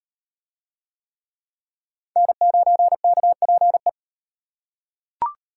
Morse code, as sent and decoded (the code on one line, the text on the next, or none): -. ----. -.- .--. .
N9KPE